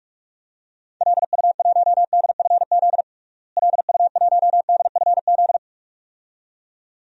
Morse code, heard as a keyed tone, about 46 wpm, 700 Hz. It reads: LU1DFZ LU1DFZ
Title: DXCC-100 Call Signs - 6 Characters - Repeated 2x - 45wpm